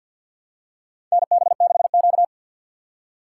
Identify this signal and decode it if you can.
Morse code, heard as a keyed tone, about 50 wpm, 700 Hz. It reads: NB6Q
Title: USA Call Signs - 4 Characters - 2x Character Spacing - 50wpm